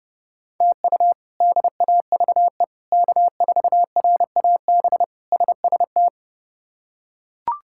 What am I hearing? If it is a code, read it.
Morse code, heard as a keyed tone, about 30 wpm, 700 Hz. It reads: TU DAVE K4RAB SST
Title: K1USN SST QSO Practice - 30wpm